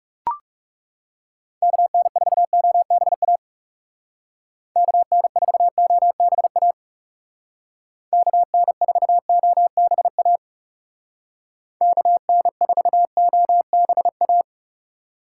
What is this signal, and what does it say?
Morse code, heard as a keyed tone, about 35 wpm, 700 Hz. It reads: KN4OBA KN4OBA KN4OBA KN4OBA
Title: Speed Racing - USA 2x3 Call Signs 30wpm